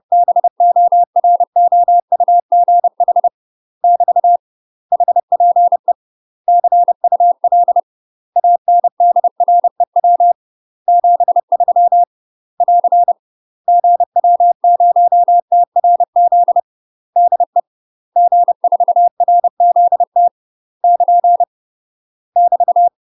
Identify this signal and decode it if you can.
Morse code, heard as a keyed tone, about 30 wpm, 700 Hz. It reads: BOROUGH <BT> HPE CUL ANDREW 73 <AR> GW0TRZ DE G4RZT <KN> <BT>
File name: UK MASTER QSOs TRAINING / 30WPM / UK MASTER QSOs 0047